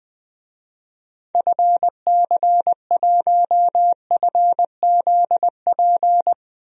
Morse code, heard as a keyed tone, about 20 wpm, 700 Hz. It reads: FC1FZP
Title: DXCC-100 Call Signs - 6 Characters - Repeated 3x - 20wpm